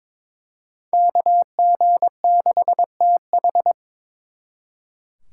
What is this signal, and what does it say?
Morse code, heard as a keyed tone, about 26 wpm, 700 Hz. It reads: KG6TH
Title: Speed Racing - USA 2x2 Call Signs 22wpm